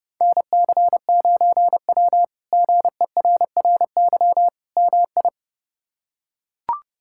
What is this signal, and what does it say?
Morse code, heard as a keyed tone, about 32 wpm, 700 Hz. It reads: NC9W GERRY MI
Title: K1USN SST QSO Practice - 30wpm